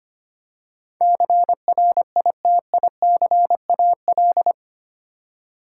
Morse code, heard as a keyed tone, about 25 wpm, 700 Hz. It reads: CRITICAL
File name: Eight Letter Words - 25wpm